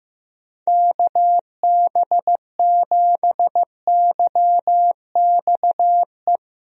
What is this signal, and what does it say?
Morse code, heard as a keyed tone, about 15 wpm, 700 Hz. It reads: KB7YXE
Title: USA Call Signs - 6 Characters - 15wpm